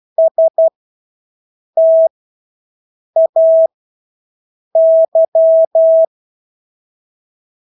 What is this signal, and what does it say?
Morse code, heard as a keyed tone, about 12 wpm, spaced as for 6 wpm, 650 Hz. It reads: STAY
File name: Engelske ord 12-6 wpm